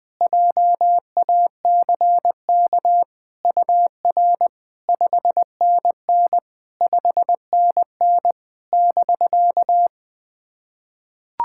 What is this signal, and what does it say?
Morse code, heard as a keyed tone, about 20 wpm, 700 Hz. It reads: JACK UR 5NN 5NN <BK>
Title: SOTA QSO Practice 20wpm